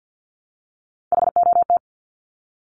Morse code, heard as a keyed tone, about 50 wpm, 700 Hz. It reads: HOT
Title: Top 500 Words Repeated 50wpm